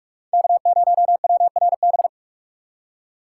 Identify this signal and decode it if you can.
Morse code, heard as a keyed tone, about 43 wpm, 700 Hz. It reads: K0WRB
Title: USA Call Signs - 5 Characters - 4x Character Spacing - 45wpm